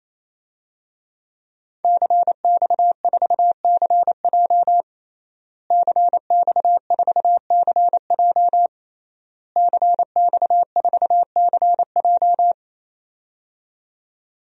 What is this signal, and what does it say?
Morse code, heard as a keyed tone, about 27 wpm, 700 Hz. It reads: CX4CJ CX4CJ CX4CJ
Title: DXCC-100 Call Signs - 5 Characters - Repeated 3x - 28wpm